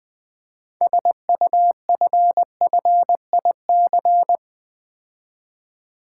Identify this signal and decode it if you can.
Morse code, heard as a keyed tone, about 21 wpm, 700 Hz. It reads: SUFFIC
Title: Build A Word - 6 Letters At A Time - 12-Letter Words - 20wpm